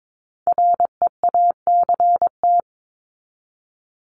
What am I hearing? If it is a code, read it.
Morse code, heard as a keyed tone, about 22 wpm, 700 Hz. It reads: REACT
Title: Top 1251-1500 Words 22wpm